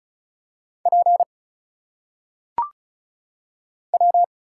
Morse code, heard as a keyed tone, about 36 wpm, 700 Hz. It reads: P W
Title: Single Letters - Phonetic Pronunciation 35wpm